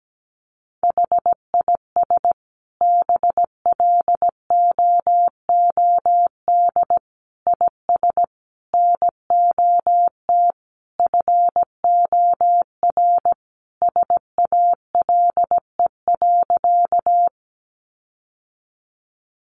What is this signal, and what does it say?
Morse code, heard as a keyed tone, about 17 wpm, 700 Hz. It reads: HIS BLOOD IS NOT FOR SALE.
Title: Sentences from Top 1400 Words 17wpm